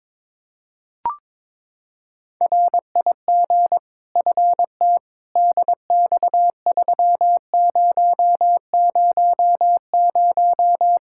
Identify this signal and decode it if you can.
Morse code, heard as a keyed tone, about 22 wpm, 700 Hz. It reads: RIG FT DX3000
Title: Practice QSO - Exchange 2 - Info - 22wpm